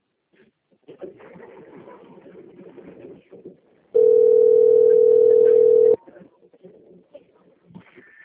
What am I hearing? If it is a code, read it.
Ringback tone (North America), 440 + 480 Hz.